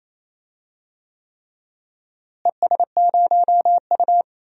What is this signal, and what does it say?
Morse code, heard as a keyed tone, about 28 wpm, 700 Hz. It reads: ES0U
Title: DXCC-100 Call Signs - 4 Characters - Repeated 3x - 28wpm